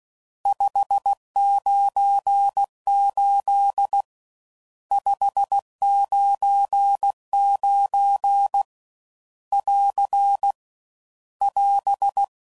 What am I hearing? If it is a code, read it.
Morse code, heard as a keyed tone, about 16 wpm, 800 Hz. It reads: 598 599 <AR> <AS>
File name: CW Code Groups @15WPM / RSTs / Various RST Nos